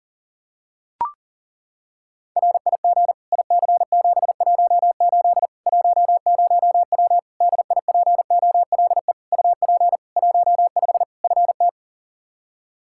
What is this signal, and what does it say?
Morse code, heard as a keyed tone, about 40 wpm, 700 Hz. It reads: RIG IC718 10W DIPOLE UP 15 FT
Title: Practice QSO - Exchange 2 - Info - 40wpm